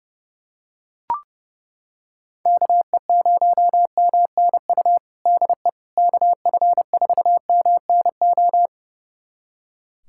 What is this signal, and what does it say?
Morse code, heard as a keyed tone, about 30 wpm, 700 Hz. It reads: KE0MNU DE KF4MNO
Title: USA Callsign DE Callsign - 6 Characters - 30wpm